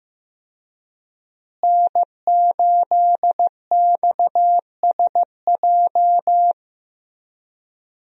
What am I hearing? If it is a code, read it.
Morse code, heard as a keyed tone, about 16 wpm, 700 Hz. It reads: N8XSJ
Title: USA Call Signs - 5 Characters - Repeated 3x - 15wpm